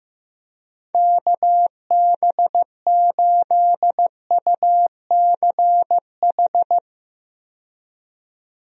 Morse code, heard as a keyed tone, about 15 wpm, 700 Hz. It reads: KB8UCH
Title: USA Call Signs - 6 Characters - 15wpm